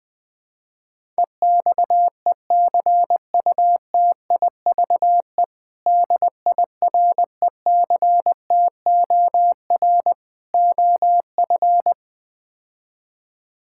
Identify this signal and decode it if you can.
Morse code, heard as a keyed tone, about 21 wpm, 700 Hz. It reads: EXECUTIVE DIRECTOR OF
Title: Build A Phrase - 4-Word Phrases - 20wpm